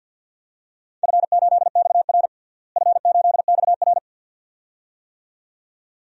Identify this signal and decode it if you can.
Morse code, heard as a keyed tone, about 51 wpm, 700 Hz. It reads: F8XR F8XR
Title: DXCC-100 Call Signs - 4 Characters - Repeated 2x - 50wpm